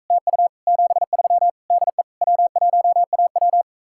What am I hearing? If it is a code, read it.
Morse code, heard as a keyed tone, about 46 wpm, 700 Hz. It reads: TU 73 DE W1AW